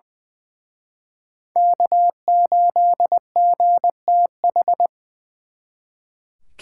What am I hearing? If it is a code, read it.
Morse code, heard as a keyed tone, about 24 wpm, 700 Hz. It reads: K8GTH
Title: Speed Racing - USA 1x3 Call Signs 20wpm